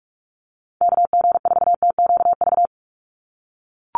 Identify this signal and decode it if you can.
Morse code, heard as a keyed tone, about 44 wpm, 700 Hz. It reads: KG4TQV
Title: USA Call Signs 45wpm